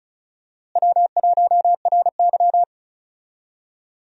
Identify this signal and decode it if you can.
Morse code, heard as a keyed tone, about 36 wpm, 700 Hz. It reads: W1RY
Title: USA Call Signs - 4 Characters - Repeated 3x - 35wpm